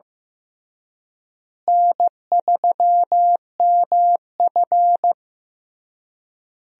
Morse code, heard as a keyed tone, about 15 wpm, 700 Hz. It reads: N3MF
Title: USA Call Signs - 4 Characters - Repeated 3x - 15wpm